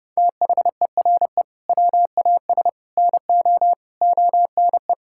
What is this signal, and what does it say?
Morse code, heard as a keyed tone, about 30 wpm, 700 Hz. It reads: THERE WAS NO ONE
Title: Sentences from Top 1600 Words 30wpm